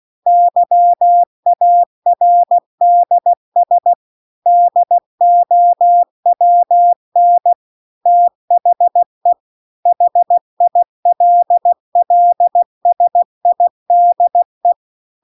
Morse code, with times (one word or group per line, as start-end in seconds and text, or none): 0.3-3.9 YARDS
4.5-7.5 DOWN
8.1-9.3 THE
9.9-14.7 HILLSIDE